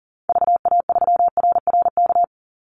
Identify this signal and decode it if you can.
Morse code, heard as a keyed tone, about 40 wpm, 700 Hz. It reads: VA3RRK
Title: Call Signs - 40wpm